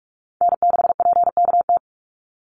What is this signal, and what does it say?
Morse code, heard as a keyed tone, about 46 wpm, 700 Hz. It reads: N6PKT
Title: NA SOTA Chaser Callsigns 45wpm